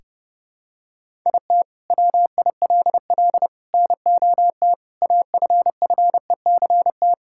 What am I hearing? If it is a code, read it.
Morse code, heard as a keyed tone, about 32 wpm, 700 Hz. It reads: IT WILL NOT AFFECT
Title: Sentences from Top 1000 Words - Review Part 6 - 30wpm